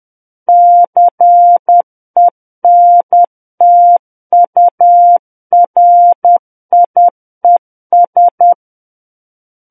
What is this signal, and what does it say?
Morse code, heard as a keyed tone, about 10 wpm, 700 Hz. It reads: CENTURIES